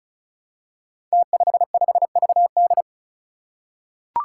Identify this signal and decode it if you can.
Morse code, heard as a keyed tone, about 36 wpm, 700 Hz. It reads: T55VD
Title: Five Challenging Letter-Number Combinations 35wpm